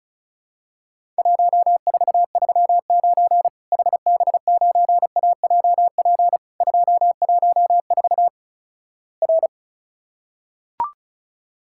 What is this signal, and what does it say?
Morse code, heard as a keyed tone, about 36 wpm, 700 Hz. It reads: 1439 HB9AJP 214 R
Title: Worked All Europe DX Contest - QTC Exchange - 35wpm